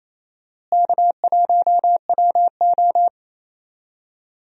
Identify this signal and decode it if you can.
Morse code, heard as a keyed tone, about 28 wpm, 700 Hz. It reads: K1WO
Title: USA Call Signs - 4 Characters - Repeated 2x - 28wpm